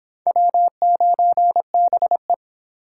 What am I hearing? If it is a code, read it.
Morse code, heard as a keyed tone, about 30 wpm, 700 Hz. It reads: W9BE